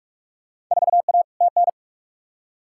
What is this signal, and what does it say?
Morse code, heard as a keyed tone, about 47 wpm, 700 Hz. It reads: 4A TN
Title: Field Day Exchanges Encore - 45wpm